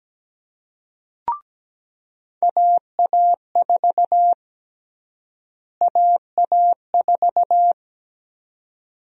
Morse code, heard as a keyed tone, about 17 wpm, 700 Hz. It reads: AA4 AA4
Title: USA Call Sign Prefixes - Repeated 2x - 17wpm